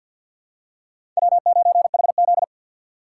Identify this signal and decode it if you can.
Morse code, heard as a keyed tone, about 52 wpm, 700 Hz. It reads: W9HZ
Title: USA Call Signs - 4 Characters - 2x Character Spacing - 50wpm